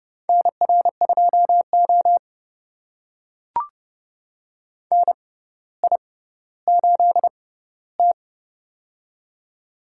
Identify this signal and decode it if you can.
Morse code, heard as a keyed tone, about 29 wpm, spaced as for 15 wpm, 700 Hz. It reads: NR2O NI8T
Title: USA Call Signs - 6x Character Spacing - 30wpm